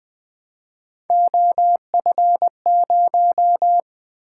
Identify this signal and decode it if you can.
Morse code, heard as a keyed tone, about 20 wpm, 700 Hz. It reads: OF0
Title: Three Letter-Number Combinations - 20wpm